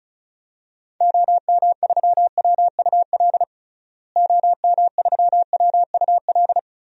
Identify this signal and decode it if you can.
Morse code, heard as a keyed tone, about 35 wpm, 700 Hz. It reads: OM3WUL OM3WUL
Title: DXCC-100 Call Signs - 6 Characters - Repeated 3x - 35wpm